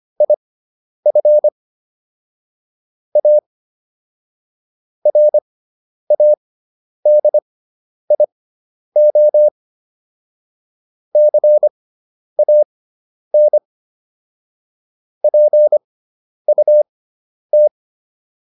Morse code, heard as a keyed tone, about 25 wpm, spaced as for 10 wpm, 600 Hz. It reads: IF A RADIO CAN PUT